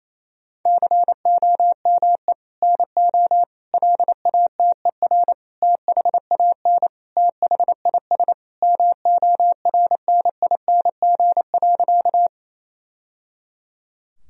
Morse code, heard as a keyed tone, about 28 wpm, 700 Hz. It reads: COME NO LATER THAN THIS MORNING.
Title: Sentences from Top 1000 Words - Review Part 4 - 28wpm